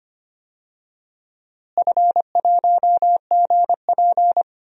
Morse code, heard as a keyed tone, about 25 wpm, 700 Hz. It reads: F1GP
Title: DXCC-100 Call Signs - 4 Characters - Repeated 3x - 25wpm